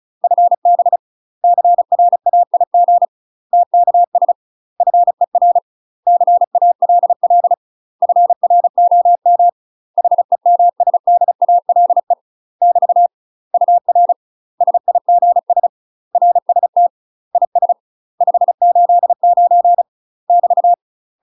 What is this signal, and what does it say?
Morse code, heard as a keyed tone, about 36 wpm, 700 Hz. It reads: FB CRAIG TKS FER CALL FROM HEMSDALE <BT> UR SIGS RST IS 589 <BT>
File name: UK MASTER QSOs TRAINING/35WPM